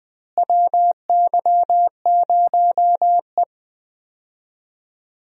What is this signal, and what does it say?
Morse code, heard as a keyed tone, about 20 wpm, 700 Hz. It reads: WY0E
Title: USA Call Signs - 4 Characters - Repeated 3x - 20wpm